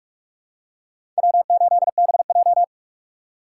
Morse code, heard as a keyed tone, about 46 wpm, 700 Hz. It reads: W8BJ
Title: USA Call Signs - 6x Character Spacing - 45wpm